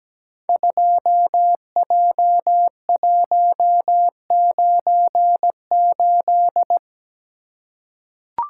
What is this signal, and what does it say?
Morse code, heard as a keyed tone, about 17 wpm, 700 Hz. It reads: 2J198
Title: Code Groups - 5 Letter:Numbers - 17wpm